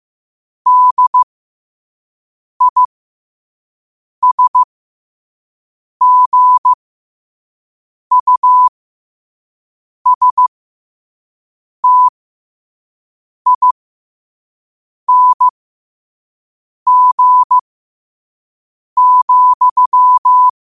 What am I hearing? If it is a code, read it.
Morse code, heard as a keyed tone, about 15 wpm, spaced as for 5 wpm, 1000 Hz. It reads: DISGUSTING,
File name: qotdfiles / 5wpm